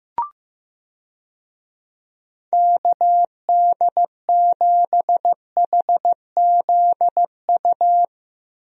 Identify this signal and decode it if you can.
Morse code, heard as a keyed tone, about 15 wpm, 700 Hz. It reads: KD7HZU
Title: USA Call Signs - 6 Characters - Repeated 3x - 15wpm